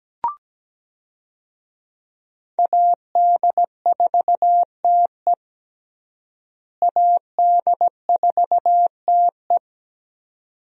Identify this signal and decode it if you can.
Morse code, heard as a keyed tone, about 17 wpm, 700 Hz. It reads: AD4TE AD4TE
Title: USA Call Signs - 5 Characters - Repeated 2x - 17wpm